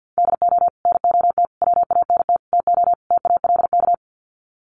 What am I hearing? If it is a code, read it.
Morse code, heard as a keyed tone, about 48 wpm, 700 Hz. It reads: DO NOT WANT TO TALK
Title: Sets of 5 Words 50wpm